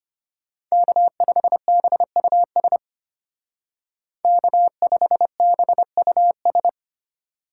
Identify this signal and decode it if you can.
Morse code, heard as a keyed tone, about 30 wpm, 700 Hz. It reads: K5BUS K5BUS
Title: Speed Racing - USA 1x3 Call Signs 25wpm